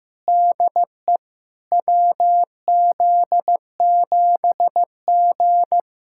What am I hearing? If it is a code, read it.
Morse code, heard as a keyed tone, about 15 wpm, 700 Hz. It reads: DE WZ7G
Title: SOTA QSO Practice 15wpm